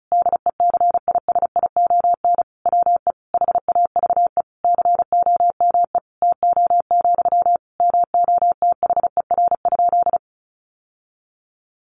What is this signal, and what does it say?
Morse code, heard as a keyed tone, about 34 wpm, 700 Hz. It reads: DECISION WE HAVE COME TO, MOTHER?
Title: Sentences from Top 1000 Words 35wpm